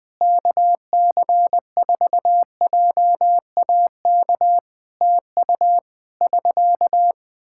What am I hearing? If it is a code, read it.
Morse code, heard as a keyed tone, about 20 wpm, 700 Hz. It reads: KC4JAK TU <SK>
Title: Practice QSO - Exchange 4 - Ending - 20wpm